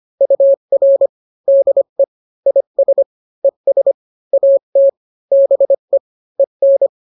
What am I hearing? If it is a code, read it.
Morse code, heard as a keyed tone, about 25 wpm, 550 Hz. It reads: UR DE IS ES AT BE EN